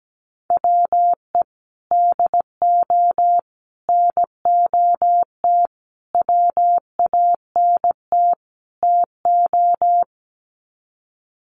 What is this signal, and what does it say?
Morse code, heard as a keyed tone, about 17 wpm, 700 Hz. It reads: WE DO NOT WANT TO
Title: Sets of 5 Words 17wpm